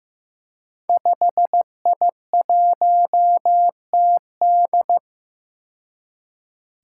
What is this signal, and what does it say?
Morse code, heard as a keyed tone, about 15 wpm, 700 Hz. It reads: 5I1TD
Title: Five Challenging Letter-Number Combinations 15wpm